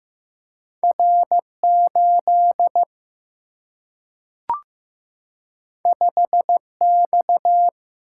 Morse code, heard as a keyed tone, about 15 wpm, 700 Hz. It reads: R8 5X
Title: Single-Letter Single-Number 15wpm